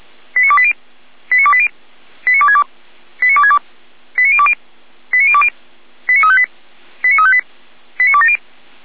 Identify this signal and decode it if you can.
Five-tone selective call (ZVEI), heard as quick runs of five tones, tones of about 70 ms. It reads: 89280 89280 89262 89262 89020 89020 89358 89358 89270